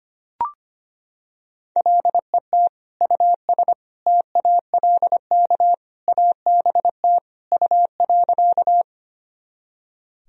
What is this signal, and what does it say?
Morse code, heard as a keyed tone, about 25 wpm, 700 Hz. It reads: LET US TALK ABT U.
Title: Sentences from Top 400 Words and Abbreviations 25wpm